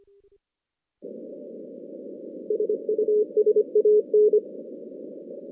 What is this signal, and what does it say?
Morse code, heard as a keyed tone, about 25 wpm, 420 Hz. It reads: SUSAN
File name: 2016 JAN CW NAQP / 2016 Jan naqp k5ot - 0-10